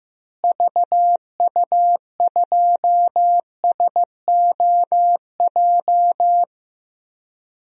Morse code, heard as a keyed tone, about 16 wpm, 700 Hz. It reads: VU2SOJ
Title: DXCC-100 Call Signs - 6 Characters - Repeated 2x - 15wpm